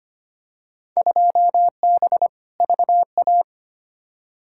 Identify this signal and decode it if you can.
Morse code, heard as a keyed tone, about 25 wpm, 700 Hz. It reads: 2B VA
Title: Field Day Exchanges Encore - 25wpm